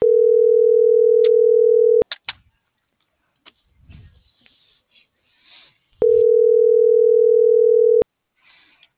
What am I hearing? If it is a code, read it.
Ringback tone (North America), 440 + 480 Hz.